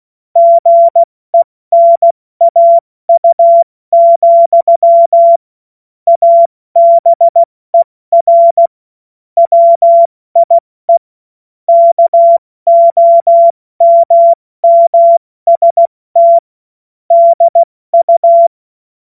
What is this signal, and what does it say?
Morse code, heard as a keyed tone, about 16 wpm, 680 Hz. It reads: GENAU, ABER WIE KOMMST DU
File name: Morsekurs / Witze / Witz 13. September 2022 70 F80